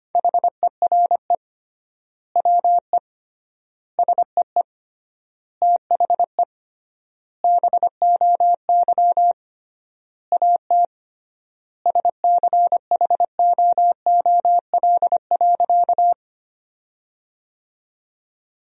Morse code, heard as a keyed tone, about 25 wpm, 700 Hz. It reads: HERE WE SEE THE BOY AT SCHOOL.
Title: 3x Word-Spacing - Sentences from Top 300 Words - 25wpm